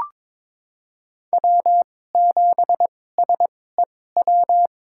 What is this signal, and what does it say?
Morse code, heard as a keyed tone, about 22 wpm, spaced as for 16 wpm, 700 Hz. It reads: W7SEW
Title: USA Call Signs - 5 Characters - 2x Character Spacing - 22wpm